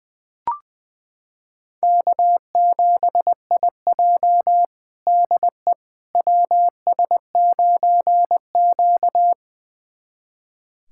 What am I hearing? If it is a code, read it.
Morse code, heard as a keyed tone, about 20 wpm, 700 Hz. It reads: K7IJ DE WS9Q
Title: USA Callsign DE Callsign - 4 Characters - 20wpm